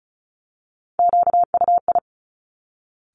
Morse code, heard as a keyed tone, about 36 wpm, 700 Hz. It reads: QUI
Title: Top Three Letter Combinations in Words 35wpm